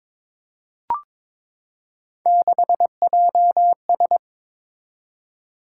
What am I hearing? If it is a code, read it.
Morse code, heard as a keyed tone, about 22 wpm, 700 Hz. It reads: 6JS
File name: practice / 22wpm / Three Letter-Number Combinations - 22wpm